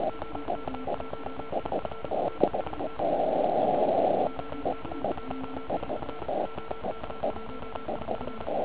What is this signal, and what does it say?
Morse code, heard as a keyed tone, about 36 wpm, 1340 Hz, with noise. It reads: OK1IF TEST OK1IF OK1IF
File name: Archiv / CUC149 / OK1IF CUC149